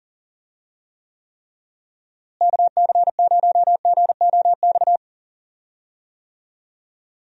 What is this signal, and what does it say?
Morse code, heard as a keyed tone, about 39 wpm, 700 Hz. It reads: KC0GOX
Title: USA Call Signs - 6 Characters - Repeated 3x - 40wpm